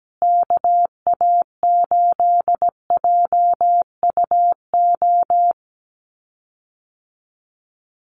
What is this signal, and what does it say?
Morse code, heard as a keyed tone, about 17 wpm, 700 Hz. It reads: KA8JUO
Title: USA Call Signs Repeated 2x 17wpm